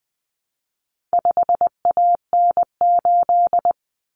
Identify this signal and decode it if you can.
Morse code, heard as a keyed tone, about 20 wpm, 700 Hz. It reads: 5AN8
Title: Common Cut Numbers - 4-Digit 20wpm